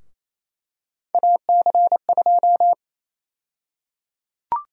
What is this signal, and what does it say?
Morse code, heard as a keyed tone, about 28 wpm, 700 Hz. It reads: AC2
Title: USA Call Sign Prefixes - 28wpm